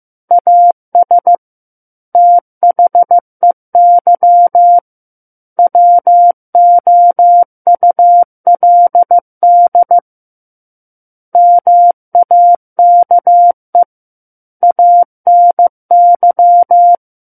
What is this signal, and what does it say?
Morse code, heard as a keyed tone, about 15 wpm, 700 Hz. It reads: AS THEY WOULD MAKE ANY
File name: Chapter 6 - 15 WPM